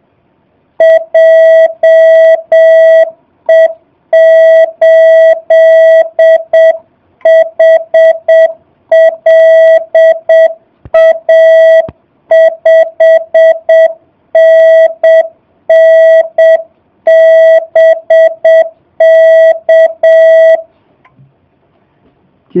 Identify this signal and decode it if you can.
Morse code, heard as a keyed tone, about 7 wpm, 650 Hz. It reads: JE8HLA 5NN BK
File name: jh8cbh / 2019cw / SP05- w